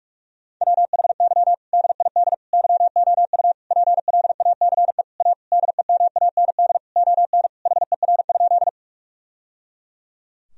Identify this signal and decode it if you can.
Morse code, heard as a keyed tone, about 46 wpm, 700 Hz. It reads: WHY DID YOU PLACE A DEMAND ON HER?
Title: Sentences from Top 1000 Words - Review Part 4 - 45wpm